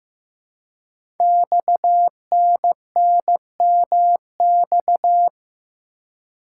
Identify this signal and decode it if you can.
Morse code, heard as a keyed tone, about 15 wpm, 700 Hz. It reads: XNNMX
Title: Code Groups - 5 Letters - 15wpm